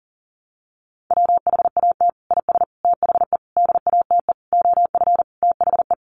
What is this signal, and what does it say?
Morse code, heard as a keyed tone, about 40 wpm, 700 Hz. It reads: WHAT IS THE DATE OF THE
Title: Sentences from Top 1100 Words 40wpm